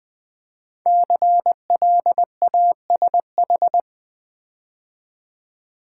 Morse code, heard as a keyed tone, about 20 wpm, 700 Hz. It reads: CLASH
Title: 5 Letter Words - First 3 Letters Given - 20wpm